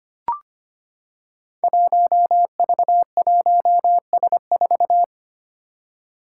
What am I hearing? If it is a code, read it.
Morse code, heard as a keyed tone, about 25 wpm, 700 Hz. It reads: 1V1S4
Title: Five Challenging Letter-Number Combinations 25wpm